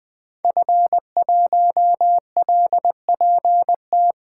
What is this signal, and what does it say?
Morse code, heard as a keyed tone, about 20 wpm, 700 Hz. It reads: F1LPT
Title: 2023 POTA Activator Call Signs - 20wpm